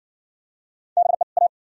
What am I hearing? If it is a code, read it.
Morse code, heard as a keyed tone, about 60 wpm, 700 Hz. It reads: BE A